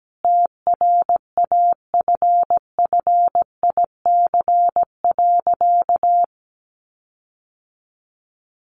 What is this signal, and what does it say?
Morse code, heard as a keyed tone, about 17 wpm, 700 Hz. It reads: TRAFFIC.
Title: Sentences from Top 1300 Words 17wpm